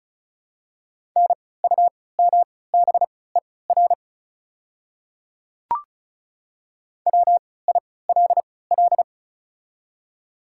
Morse code, heard as a keyed tone, about 36 wpm, spaced as for 20 wpm, 690 Hz. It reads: NUMBER WILL
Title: 3x Character Spacing - Top 100 Words - 35wpm